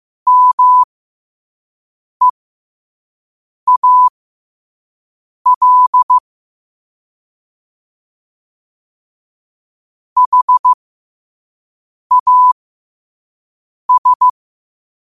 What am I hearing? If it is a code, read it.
Morse code, heard as a keyed tone, about 15 wpm, spaced as for 5 wpm, 1000 Hz. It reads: MEAL HAS
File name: qotdfiles / 5wpm